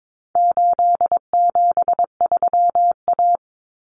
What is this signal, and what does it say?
Morse code, heard as a keyed tone, about 22 wpm, 700 Hz. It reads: 873A
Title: Common Cut Numbers - 4-Digit 22wpm